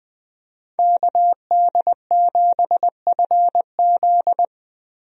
Morse code, heard as a keyed tone, about 21 wpm, 700 Hz. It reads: KD7FZ
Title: USA Call Signs - 5 Characters - 20wpm